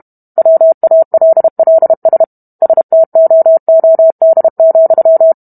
Morse code, heard as a keyed tone, about 32 wpm, 650 Hz. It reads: WALLS STOOD,